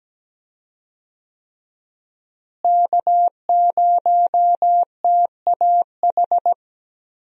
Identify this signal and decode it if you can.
Morse code, heard as a keyed tone, about 17 wpm, 700 Hz. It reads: K0TAH
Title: USA Call Signs - 5 Characters - Repeated 3x - 17wpm